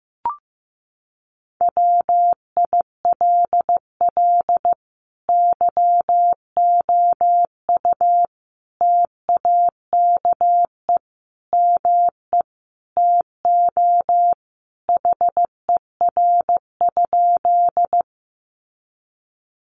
Morse code, heard as a keyed tone, about 15 wpm, 700 Hz. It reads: WILL YOU TAKE ME TO HER?
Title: Sentences from Top 200 Words 15wpm